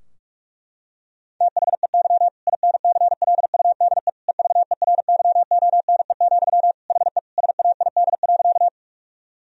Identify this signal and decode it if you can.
Morse code, heard as a keyed tone, about 46 wpm, 700 Hz. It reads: THEY INCLUDE EVERYONE, HE SAID.
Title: Sentences from Top 1000 Words - Review Part 3 - 45wpm